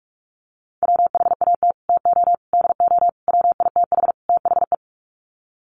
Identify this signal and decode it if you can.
Morse code, heard as a keyed tone, about 44 wpm, 700 Hz. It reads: WHAT TO DO WITH THE
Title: Sets of 5 Words Encore2 45wpm